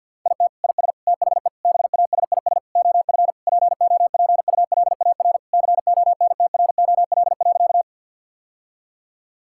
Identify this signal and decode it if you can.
Morse code, heard as a keyed tone, about 52 wpm, 690 Hz. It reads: IT IS THE BASIS OF POPULAR CONTROL.
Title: Sentences from Top 1000 Words - Review Part 5 - 50wpm